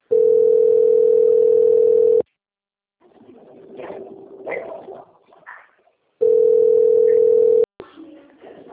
Ringback tone (North America), 440 + 480 Hz.